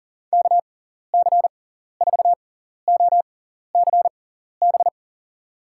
Morse code, heard as a keyed tone, about 40 wpm, spaced as for 14 wpm, 700 Hz. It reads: KC4OCB
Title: USA Call Signs - 6 Characters - 6x Character Spacing - 40wpm